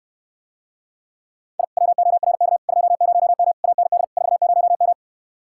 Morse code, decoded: I WORK 20M AND 40M

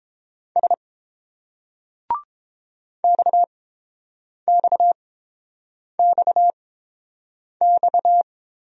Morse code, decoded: S XXXX

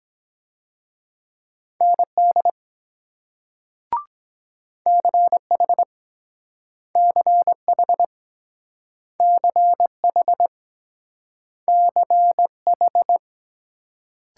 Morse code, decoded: ND CH CH CH CH

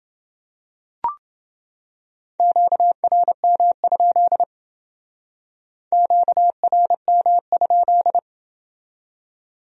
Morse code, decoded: QRM? QRM?

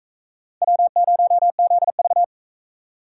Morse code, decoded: W0ZV

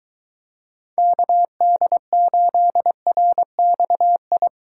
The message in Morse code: KD8RXI

